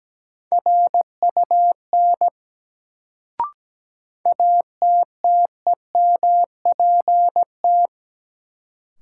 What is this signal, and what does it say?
Morse code, heard as a keyed tone, about 17 wpm, 700 Hz. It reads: RUN ATTEMPT
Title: Sets of 4 Words - New Word Drill - 17wpm